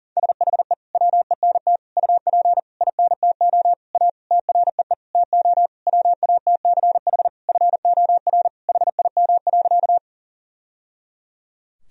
Morse code, heard as a keyed tone, about 37 wpm, 700 Hz. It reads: SHE WENT UP INTO A TREE TO WATCH FOR HIM.
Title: Sentences from Top 1000 Words - Review Part 1 - 40wpm